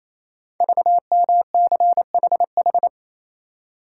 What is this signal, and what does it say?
Morse code, heard as a keyed tone, about 28 wpm, 700 Hz. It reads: VMCHH
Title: Code Groups - 5 Letter:Numbers - 28wpm